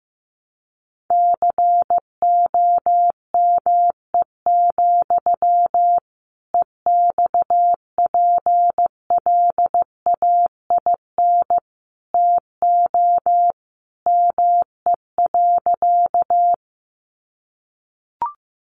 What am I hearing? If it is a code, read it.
Morse code, heard as a keyed tone, about 15 wpm, 700 Hz. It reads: COME, EXPLAIN TO ME.
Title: Sentences from Top 1100 Words 15wpm